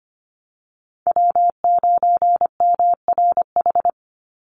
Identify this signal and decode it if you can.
Morse code, heard as a keyed tone, about 25 wpm, 700 Hz. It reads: W9MRH
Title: NA SOTA Chaser Callsigns 25wpm